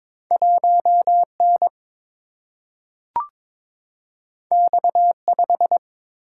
Morse code, heard as a keyed tone, about 22 wpm, 700 Hz. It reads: 1 N X 5
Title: Single-Letter Single-Number 22wpm